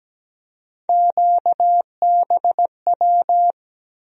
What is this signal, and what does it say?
Morse code, heard as a keyed tone, about 17 wpm, 700 Hz. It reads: QBW
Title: USA Call Sign Postfixes - 3 Letters 17wpm